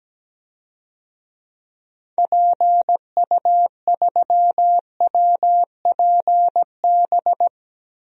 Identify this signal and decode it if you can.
Morse code, heard as a keyed tone, about 17 wpm, 700 Hz. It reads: PU3WPB
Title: DXCC-100 Call Signs - 6 Characters - Repeated 3x - 17wpm